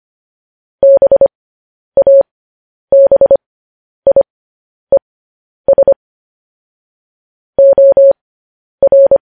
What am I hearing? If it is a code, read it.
Morse code, heard as a keyed tone, about 25 wpm, spaced as for 10 wpm, 550 Hz. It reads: BABIES OR
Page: News Headlines in Morse Code at 10 WPM